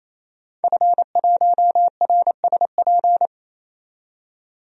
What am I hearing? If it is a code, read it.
Morse code, heard as a keyed tone, about 28 wpm, 700 Hz. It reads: F1RSP
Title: DXCC-100 Call Signs - 5 Characters - 28wpm